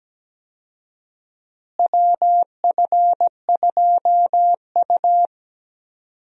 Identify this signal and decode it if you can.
Morse code, heard as a keyed tone, about 17 wpm, 700 Hz. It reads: WF2U
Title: USA Call Signs - 4 Characters - Repeated 3x - 17wpm